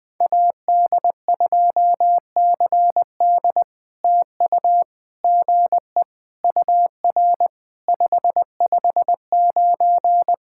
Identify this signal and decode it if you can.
Morse code, heard as a keyed tone, about 19 wpm, 700 Hz. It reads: AD2CD TU GE UR 559
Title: POTA QSO Practice - 20wpm